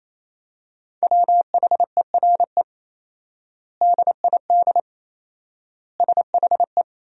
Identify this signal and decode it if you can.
Morse code, heard as a keyed tone, about 28 wpm, 700 Hz. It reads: WHERE DID SHE
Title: 4x Word-Spacing - Sentences from Top 200 Words - 28wpm